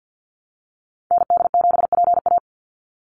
Morse code, heard as a keyed tone, about 48 wpm, 700 Hz. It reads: ND7PA
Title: NA SOTA Activator Callsigns 50wpm